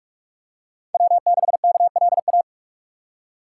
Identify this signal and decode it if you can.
Morse code, heard as a keyed tone, about 44 wpm, 700 Hz. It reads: W6KLA